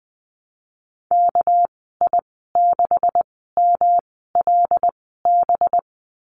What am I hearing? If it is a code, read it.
Morse code, heard as a keyed tone, about 20 wpm, spaced as for 14 wpm, 700 Hz. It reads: KI6MLB
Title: 219 Lesson 35g - Number 0 - Call Signs 20wpm